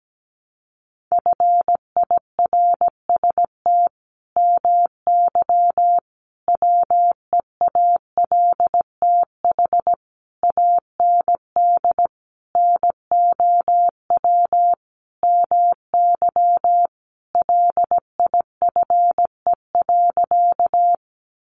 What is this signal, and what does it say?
Morse code, heard as a keyed tone, about 17 wpm, 700 Hz. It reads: FIRST MY WEALTH AND NOW MY LIFE.
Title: Sentences from Top 1200 Words 17wpm